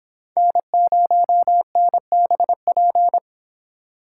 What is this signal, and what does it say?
Morse code, heard as a keyed tone, about 32 wpm, 700 Hz. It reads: N0NBP